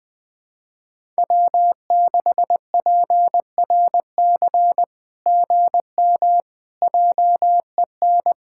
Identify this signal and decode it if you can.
Morse code, heard as a keyed tone, about 19 wpm, 700 Hz. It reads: W6PRC GM JEN
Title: SOTA QSO Practice 20wpm